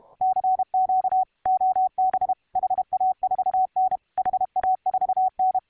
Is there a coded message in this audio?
Yes: Morse code from 0.2 to 5.6 s, heard as a keyed tone, about 31 wpm, 750 Hz, with radio noise. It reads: CQ OB HA4N HA4N